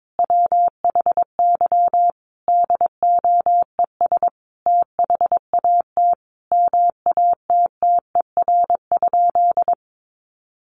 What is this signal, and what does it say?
Morse code, heard as a keyed tone, about 22 wpm, 700 Hz. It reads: WHY DOES THAT MATTER?
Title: Sentences from Top 700 Words 22wpm